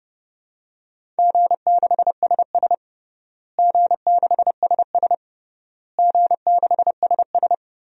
Morse code, heard as a keyed tone, about 30 wpm, 700 Hz. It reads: G6SS G6SS G6SS
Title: DXCC-100 Call Signs - 4 Characters - Repeated 3x - 30wpm